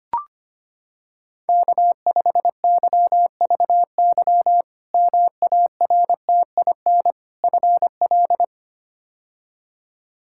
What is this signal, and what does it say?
Morse code, heard as a keyed tone, about 25 wpm, 700 Hz. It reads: K5YVY MARTIN FL
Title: K1USN SST QSO Practice - 25wpm